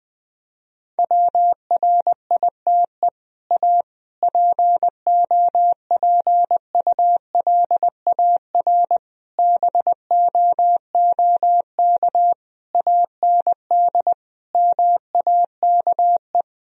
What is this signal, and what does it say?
Morse code, heard as a keyed tone, about 20 wpm, 700 Hz. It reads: WRITE A POPULAR BOOK AND MAKE
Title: Sentences from Top 1000 Words - Review Part 3 - 20wpm